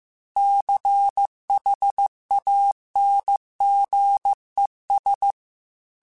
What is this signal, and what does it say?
Morse code, heard as a keyed tone, about 15 wpm, 800 Hz. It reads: CHANGES
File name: Technical Subjects / Technical Subjects-mp3s / Reactance @13